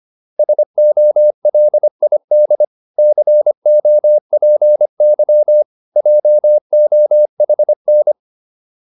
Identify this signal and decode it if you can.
Morse code, heard as a keyed tone, about 25 wpm, 600 Hz. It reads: SOLID COPY JOHN
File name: QSO305 25